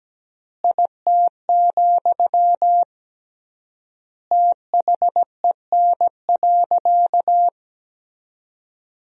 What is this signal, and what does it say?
Morse code, heard as a keyed tone, about 17 wpm, 700 Hz. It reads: IT, THEN.
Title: 3x Word-Spacing - Sentences from Top 200 Words - 17wpm